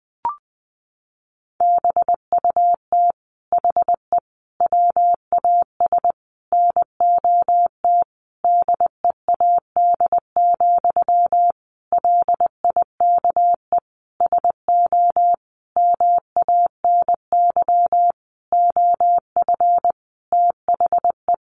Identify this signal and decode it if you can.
Morse code, heard as a keyed tone, about 20 wpm, 700 Hz. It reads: BUT HE WAS NOT DEAD, LIKE SO MANY OF THE
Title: Sentences from Top 1500 Words 20wpm